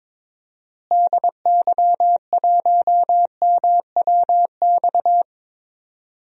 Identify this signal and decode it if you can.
Morse code, heard as a keyed tone, about 22 wpm, 700 Hz. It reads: DY1MWX
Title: DXCC-100 Call Signs - 6 Characters - Repeated 3x - 22wpm